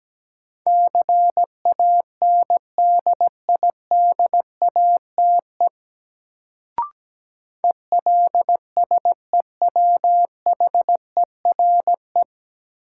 Morse code, heard as a keyed tone, about 17 wpm, 700 Hz. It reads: CANDIDATE ELSEWHERE
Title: Sentences from Top 1600 Words - New Word Drill Alt - 17wpm